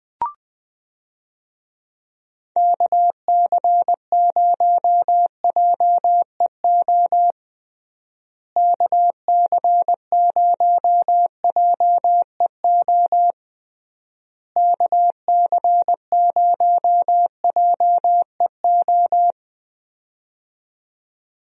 Morse code, heard as a keyed tone, about 20 wpm, 700 Hz. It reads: KC0JEO KC0JEO KC0JEO
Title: USA Call Signs - 6 Characters - Repeated 3x - 20wpm